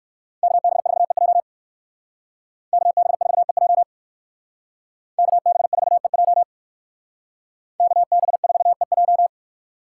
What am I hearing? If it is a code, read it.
Morse code, heard as a keyed tone, about 56 wpm, 700 Hz. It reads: KB4EJ KB4EJ KB4EJ KB4EJ
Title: Speed Racing - USA 2x2 Call Signs 45wpm